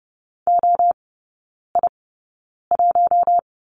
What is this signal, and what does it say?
Morse code, heard as a keyed tone, about 30 wpm, spaced as for 9 wpm, 700 Hz. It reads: OI1
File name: course / 30:10wpm / 087 Lesson 15e - Number 5 - Three Characters 30wpm